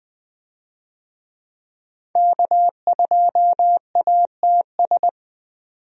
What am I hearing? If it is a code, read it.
Morse code, heard as a keyed tone, about 20 wpm, 700 Hz. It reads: K2ATS
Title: USA Call Signs - 5 Characters - 20wpm